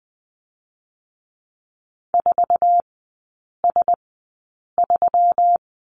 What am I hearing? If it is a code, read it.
Morse code, heard as a keyed tone, about 20 wpm, spaced as for 8 wpm, 700 Hz. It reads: 4S3
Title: 133 Lesson 22e - Question Mark - Three Characters 20wpm